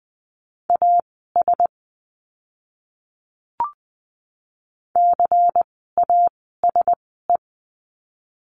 Morse code, heard as a keyed tone, about 20 wpm, spaced as for 14 wpm, 700 Hz. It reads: AS CASE
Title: 101 Lesson 17f - Letter u - Words 20wpm